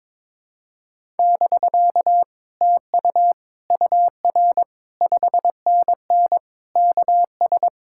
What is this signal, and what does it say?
Morse code, heard as a keyed tone, about 21 wpm, 700 Hz. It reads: <BK> TU UR 5NN KS
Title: Speed-Racing Common QSO Phrases Encore - 22wpm